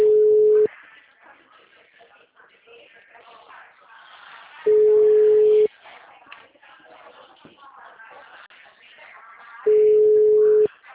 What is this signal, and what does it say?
Ringback tone (Europe, 425 Hz), with speech.